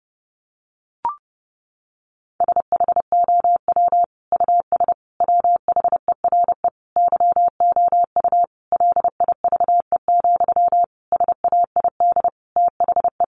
Morse code, heard as a keyed tone, about 30 wpm, 700 Hz. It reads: SHOW US WHERE YOU LIVE, SAID THE
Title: Sentences from Top 200 Words 30wpm